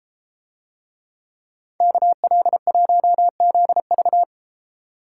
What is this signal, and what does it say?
Morse code, heard as a keyed tone, about 32 wpm, 700 Hz. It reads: KL1ZV